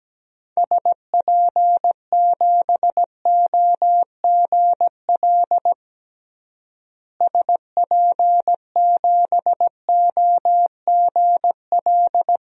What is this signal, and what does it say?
Morse code, heard as a keyed tone, about 17 wpm, 700 Hz. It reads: SP7OGL SP7OGL
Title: DXCC-100 Call Signs - 6 Characters - Repeated 2x - 17wpm